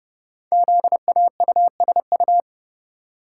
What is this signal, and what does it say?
Morse code, heard as a keyed tone, about 30 wpm, 700 Hz. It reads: ZAUSU
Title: Code Groups - 5 Letters - 30wpm